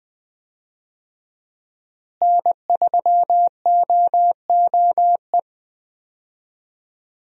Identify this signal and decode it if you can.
Morse code, heard as a keyed tone, about 20 wpm, 700 Hz. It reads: N3OOE
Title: USA Call Signs - 5 Characters - Repeated 2x - 20wpm